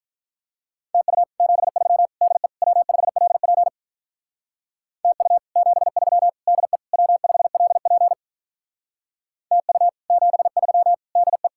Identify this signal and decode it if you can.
Morse code, heard as a keyed tone, about 46 wpm, 700 Hz. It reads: TU 73 DE W5LP TU 73 DE W5LP TU 73 DE